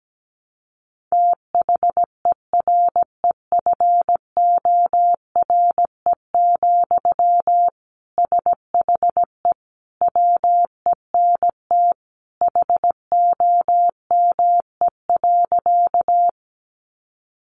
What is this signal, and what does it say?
Morse code, heard as a keyed tone, about 17 wpm, 700 Hz. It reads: THEREFORE, SHE WENT HOME.
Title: Sentences from Top 800 Words 17wpm